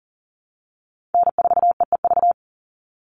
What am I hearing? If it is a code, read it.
Morse code, heard as a keyed tone, about 40 wpm, 700 Hz. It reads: N4EEV
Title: Call Signs Repeated 3x - 40wpm